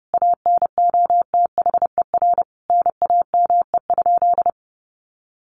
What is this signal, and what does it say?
Morse code, heard as a keyed tone, about 30 wpm, 700 Hz. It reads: ANOTHER NAME?
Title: Sentences from Top 600 Words 30wpm